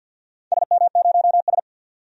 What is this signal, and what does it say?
Morse code, heard as a keyed tone, about 47 wpm, 700 Hz. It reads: SM0S
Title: DXCC-100 Call Signs - 4 Characters - 50wpm